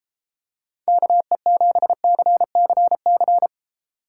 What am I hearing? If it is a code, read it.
Morse code, heard as a keyed tone, about 33 wpm, 700 Hz. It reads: KE7CCC